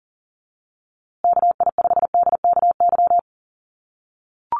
Morse code, heard as a keyed tone, about 40 wpm, 700 Hz. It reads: KI5DKY
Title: USA Call Signs Repeated 2x 40wpm